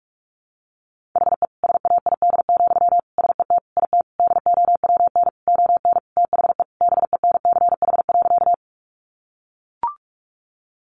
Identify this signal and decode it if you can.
Morse code, heard as a keyed tone, about 44 wpm, 700 Hz. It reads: HE SAID, SET IT DOWN ON THE BENCH.
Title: Sentences from Top 1400 Words 45wpm